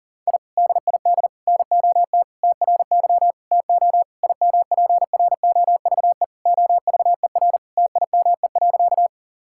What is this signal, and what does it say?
Morse code, heard as a keyed tone, about 40 wpm, 700 Hz. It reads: I DID NOT TRY TO IMPROVE OVER TIME.
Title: Sentences from Top 1000 Words - Review Part 5 - 40wpm